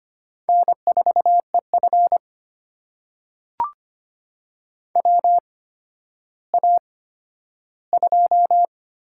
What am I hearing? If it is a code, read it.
Morse code, heard as a keyed tone, about 25 wpm, spaced as for 12 wpm, 700 Hz. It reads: N4EF WA2
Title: USA Call Signs - 4 Characters - 8x Character Spacing - 25wpm